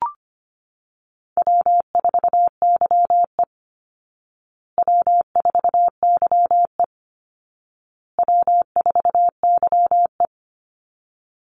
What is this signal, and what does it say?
Morse code, heard as a keyed tone, about 25 wpm, 700 Hz. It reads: W4YE W4YE W4YE
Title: Easy Call Signs Repeated 3x - 25wpm